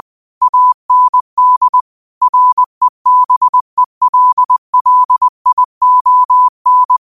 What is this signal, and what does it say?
Morse code, heard as a keyed tone, about 20 wpm, 1000 Hz. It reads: AND REBELLION